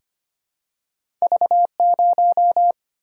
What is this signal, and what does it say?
Morse code, heard as a keyed tone, about 25 wpm, 700 Hz. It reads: V0
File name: Two Letter-Number Combinations - 25wpm